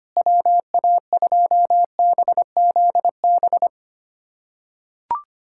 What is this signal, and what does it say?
Morse code, heard as a keyed tone, about 25 wpm, 700 Hz. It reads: WA2BZB
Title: USA Callsign DE Callsign - 6 Characters - 25wpm